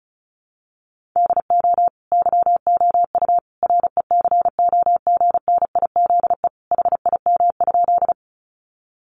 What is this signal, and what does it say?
Morse code, heard as a keyed tone, about 34 wpm, 700 Hz. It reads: DO YOU RECOGNIZE HIM?
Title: Sentences from Top 1400 Words 35wpm